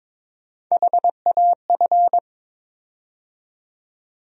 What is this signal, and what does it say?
Morse code, heard as a keyed tone, about 22 wpm, 700 Hz. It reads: HAF
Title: USA Call Sign Postfixes - 3 Letters 22wpm